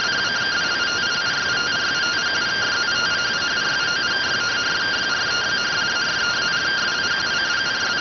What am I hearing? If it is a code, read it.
Two-tone teleprinter signal, about 170 Hz shift.